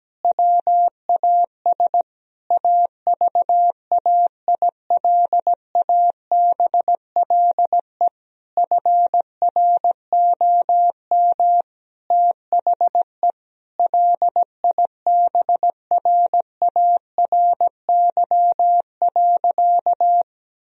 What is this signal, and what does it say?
Morse code, heard as a keyed tone, about 17 wpm, 700 Hz. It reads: WAS AVAILABLE FROM THE LIBRARY.
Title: Sentences from Top 1000 Words - Review Part 3 - 17wpm